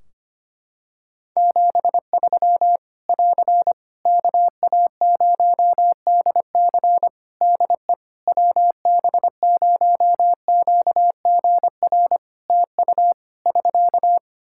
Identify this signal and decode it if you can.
Morse code, heard as a keyed tone, about 25 wpm, 700 Hz. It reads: 73 <AR> KA0DC DE WB0QGR TU <SK>
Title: Practice QSO - Exchange 4 - Ending - 25wpm